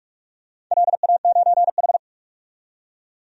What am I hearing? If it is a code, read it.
Morse code, heard as a keyed tone, about 46 wpm, 700 Hz. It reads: LA9H
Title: DXCC-100 Call Signs - 4 Characters - 45wpm